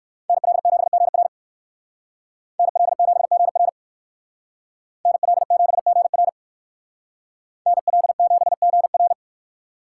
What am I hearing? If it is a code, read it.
Morse code, heard as a keyed tone, about 65 wpm, 700 Hz. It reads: NL7GR NL7GR NL7GR NL7GR